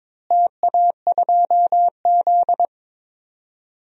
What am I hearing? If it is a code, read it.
Morse code, heard as a keyed tone, about 22 wpm, 700 Hz. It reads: TA2Z
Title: DXCC-100 Call Signs - 4 Characters - Repeated 3x - 22wpm